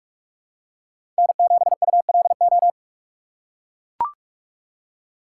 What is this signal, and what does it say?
Morse code, heard as a keyed tone, about 44 wpm, 700 Hz. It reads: N7ULO
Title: USA Call Signs - 5 Characters - 3x Character Spacing - 45wpm